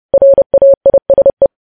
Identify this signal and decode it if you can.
Morse code, heard as a keyed tone, about 30 wpm, 550 Hz. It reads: RAISE
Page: news headlines in morse code at 30 wpm